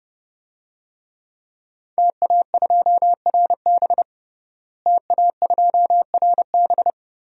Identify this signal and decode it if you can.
Morse code, heard as a keyed tone, about 29 wpm, 700 Hz. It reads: TA2RB TA2RB
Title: DXCC-100 Call Signs - 5 Characters - Repeated 2x - 30wpm